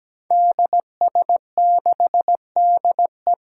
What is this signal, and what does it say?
Morse code, heard as a keyed tone, about 17 wpm, 700 Hz. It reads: DS6DE
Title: Five Challenging Letter-Number Combinations 17wpm